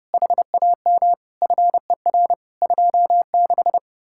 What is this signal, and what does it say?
Morse code, heard as a keyed tone, about 36 wpm, 700 Hz. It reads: HAM FER 26
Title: Speed-Racing Common QSO Phrases - 30wpm